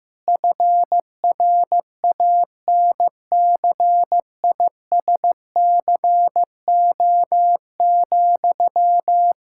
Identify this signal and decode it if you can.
Morse code, heard as a keyed tone, about 16 wpm, 700 Hz. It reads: FRANCISCO,
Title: Practice QSO - Exchange 1 - Beginning - 15wpm